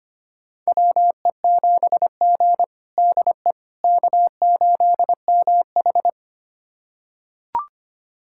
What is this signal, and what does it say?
Morse code, heard as a keyed tone, about 25 wpm, 700 Hz. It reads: WE7G DE K8MH
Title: USA Callsign DE Callsign - 4 Characters - 25wpm